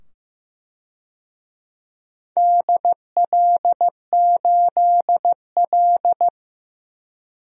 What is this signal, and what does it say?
Morse code, heard as a keyed tone, about 16 wpm, 700 Hz. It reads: DL8L
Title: DXCC-100 Call Signs - 4 Characters - 15wpm